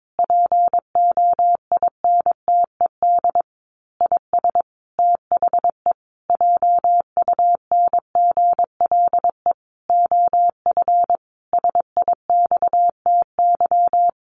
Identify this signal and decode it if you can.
Morse code, heard as a keyed tone, about 22 wpm, 700 Hz. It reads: POINTED IS THE JUNGLE OF SIXTY
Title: A Sound Of Thunder 22wpm